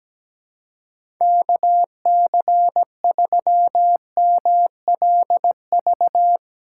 Morse code, heard as a keyed tone, about 17 wpm, 700 Hz. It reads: KC3MLV
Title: USA Call Signs - 6 Characters - 4x Character Spacing - 17wpm